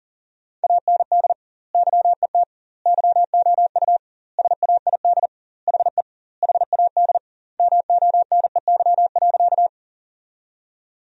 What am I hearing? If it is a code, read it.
Morse code, heard as a keyed tone, about 40 wpm, 700 Hz. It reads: AND YET YOU SAID HE HAD MONEY.
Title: 2x Word-Spacing - Sentences from Top 500 Words - 40wpm